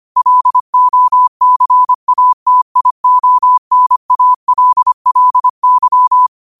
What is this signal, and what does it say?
Morse code, heard as a keyed tone, about 25 wpm, 1000 Hz. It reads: LOCATIONALLY
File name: qotdfiles / 25wpm